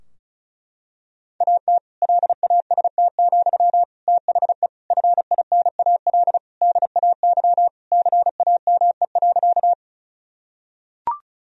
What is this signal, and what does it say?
Morse code, heard as a keyed tone, about 36 wpm, 700 Hz. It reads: AT LAST, THE FINAL DAY CAME.
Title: Sentences from Top 1000 Words - Review Part 2 - 35wpm